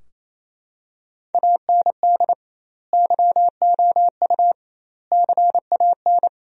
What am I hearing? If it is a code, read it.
Morse code, heard as a keyed tone, about 28 wpm, 700 Hz. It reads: AND YOU CAN
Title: 2x Word-Spacing - Sentences from Top 400 Words - 28wpm